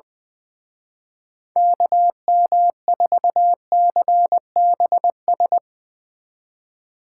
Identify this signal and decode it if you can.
Morse code, heard as a keyed tone, about 20 wpm, 700 Hz. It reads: KM4CBS
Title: Speed Racing - USA 2x3 Call Signs 20wpm